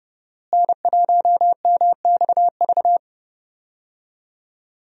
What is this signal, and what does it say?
Morse code, heard as a keyed tone, about 31 wpm, 700 Hz. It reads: N1MXV
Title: USA Call Signs - 5 Characters - Repeated 3x - 30wpm